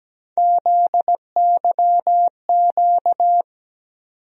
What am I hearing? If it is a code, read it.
Morse code, heard as a keyed tone, about 17 wpm, 700 Hz. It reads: ZYQ
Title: USA Call Sign Postfixes - 3 Letters 17wpm